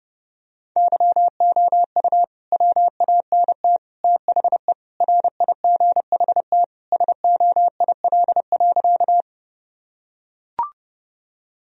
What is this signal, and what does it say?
Morse code, heard as a keyed tone, about 32 wpm, 700 Hz. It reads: YOU WANT THE RIGHT SOIL.
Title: Sentences from Top 1000 Words - Review Part 4 - 30wpm